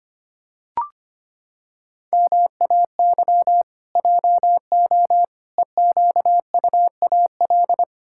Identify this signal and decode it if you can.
Morse code, heard as a keyed tone, about 25 wpm, 700 Hz. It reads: MAY JO EQUAL
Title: Pangram Challenge v2 - 25wpm